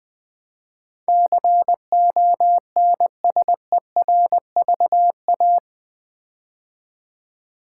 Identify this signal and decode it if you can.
Morse code, heard as a keyed tone, about 20 wpm, 700 Hz. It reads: CONSERVA
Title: Build A Word - 4 Letters At A Time - 12-Letter Words - 20wpm